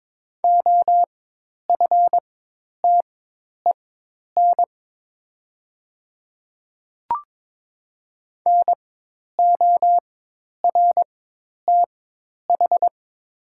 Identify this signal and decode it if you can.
Morse code, heard as a keyed tone, about 22 wpm, spaced as for 10 wpm, 700 Hz. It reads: OFTEN NORTH
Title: 4x Character Spacing - Top 300 Words - 22wpm